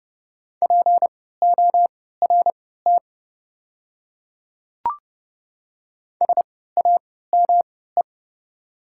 Morse code, heard as a keyed tone, about 29 wpm, spaced as for 17 wpm, 700 Hz. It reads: PORT SAME
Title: 3x Character Spacing - Top 200 Words - 30wpm